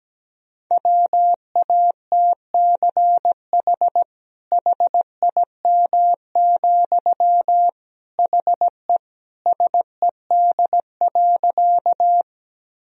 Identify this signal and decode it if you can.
Morse code, heard as a keyed tone, about 17 wpm, 700 Hz. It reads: WATCH HIM, HE SED.
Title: Sentences from Top 300 Words and Abbreviations 17wpm